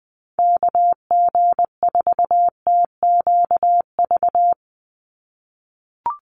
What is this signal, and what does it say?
Morse code, heard as a keyed tone, about 20 wpm, 700 Hz. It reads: KG4TQV
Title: USA Call Signs 20wpm